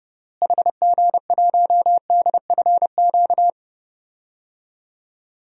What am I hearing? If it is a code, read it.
Morse code, heard as a keyed tone, about 32 wpm, 700 Hz. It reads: HG1DFQ
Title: DXCC-100 Call Signs - 6 Characters - Repeated 3x - 30wpm